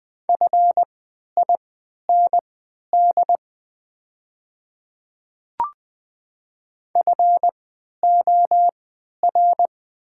Morse code, heard as a keyed tone, about 20 wpm, spaced as for 11 wpm, 700 Hz. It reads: FIND FOR
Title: 3x Character Spacing - Top 100 Words - 20wpm